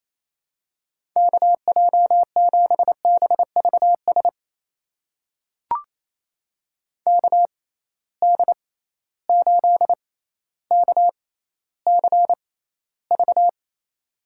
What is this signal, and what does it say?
Morse code, heard as a keyed tone, about 27 wpm, spaced as for 14 wpm, 700 Hz. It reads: KJ7BVS KD8KCV
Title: USA Call Signs - 6 Characters - 6x Character Spacing - 28wpm